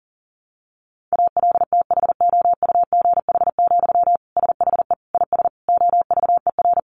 Morse code, heard as a keyed tone, about 40 wpm, 700 Hz. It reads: ALTHOUGH, SHE IS OVER
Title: Sentences from Top 600 Words 40wpm